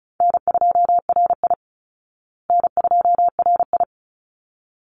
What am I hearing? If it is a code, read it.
Morse code, heard as a keyed tone, about 34 wpm, 700 Hz. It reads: N2RI N2RI
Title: Easy Call Signs Repeated 2x - 35wpm